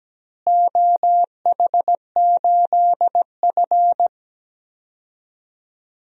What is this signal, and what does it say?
Morse code, heard as a keyed tone, about 17 wpm, 700 Hz. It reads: OH8F
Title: DXCC-100 Call Signs - 4 Characters - Repeated 3x - 17wpm